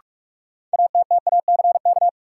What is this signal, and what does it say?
Morse code, heard as a keyed tone, about 45 wpm, 700 Hz. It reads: ATTACK
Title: Sentences from Top 1500 Words - New Word Drill - 45wpm